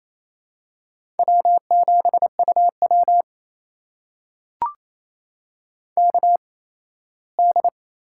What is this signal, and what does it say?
Morse code, heard as a keyed tone, about 28 wpm, spaced as for 17 wpm, 700 Hz. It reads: W7UW KD
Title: USA Call Signs - 4 Characters - 8x Character Spacing - 28wpm